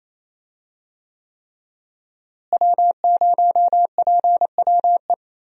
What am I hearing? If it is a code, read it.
Morse code, heard as a keyed tone, about 28 wpm, 700 Hz. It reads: W0PWE
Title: USA Call Signs - 5 Characters - Repeated 3x - 28wpm